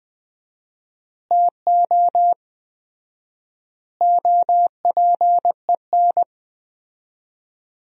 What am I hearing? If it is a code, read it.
Morse code, heard as a keyed tone, about 21 wpm, 700 Hz. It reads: TO OPEN
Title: 4x Word-Spacing - Sentences from Top 300 Words - 20wpm